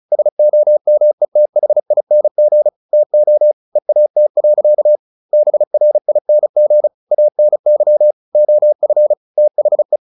Morse code, heard as a keyed tone, about 34 wpm, 600 Hz. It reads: SOMETHING TO EAT. BRING ANY OF THE
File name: the old man and the sea 2 35wpm